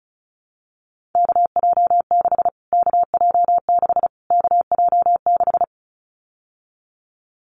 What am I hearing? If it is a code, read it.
Morse code, heard as a keyed tone, about 34 wpm, 700 Hz. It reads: KJ6 KJ6 KJ6
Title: Callsign Prefixes for Top 25 DXCC - Repeated 3x - 35wpm